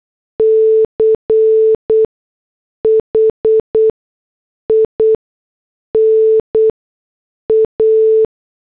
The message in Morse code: CHINA